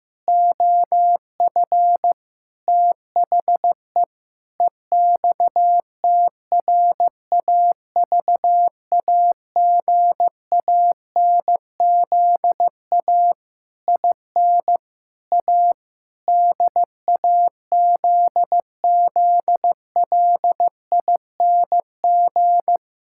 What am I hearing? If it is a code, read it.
Morse code, heard as a keyed tone, about 15 wpm, 700 Hz. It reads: OF THE EXTRAVAGANZA IN A DAZZLING
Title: Pangram Challenge v2 - 15wpm